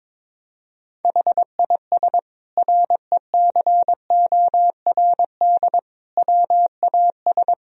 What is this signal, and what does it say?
Morse code, heard as a keyed tone, about 22 wpm, 700 Hz. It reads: HIS RECORD WAS
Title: Sentences from Top 1000 Words - Review Part 3 - 22wpm